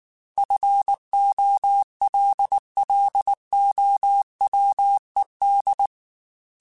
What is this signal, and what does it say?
Morse code, heard as a keyed tone, about 19 wpm, 800 Hz. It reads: FOLLOWED
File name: Technical Subjects / Technical Subjects-mp3s / Sw Pwr Sup @18